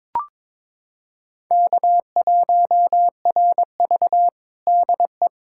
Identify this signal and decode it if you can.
Morse code, heard as a keyed tone, about 22 wpm, 700 Hz. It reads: K1RV DE
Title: Practice QSO - Exchange 4 - Ending - 22wpm